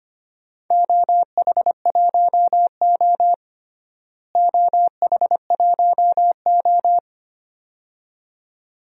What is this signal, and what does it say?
Morse code, heard as a keyed tone, about 25 wpm, 700 Hz. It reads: OH1O OH1O
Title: DXCC-100 Call Signs - 4 Characters - Repeated 2x - 25wpm